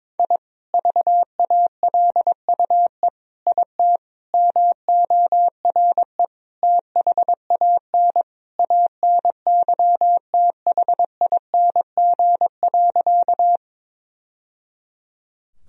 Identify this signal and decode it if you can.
Morse code, heard as a keyed tone, about 22 wpm, 700 Hz. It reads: I VALUE IT MORE THAN ANYTHING.
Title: Sentences from Top 1000 Words - Review Part 3 - 22wpm